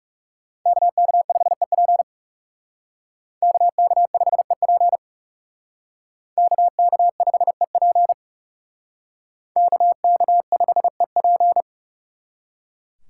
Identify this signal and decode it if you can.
Morse code, heard as a keyed tone, about 38 wpm, 700 Hz. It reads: KK5EP KK5EP KK5EP KK5EP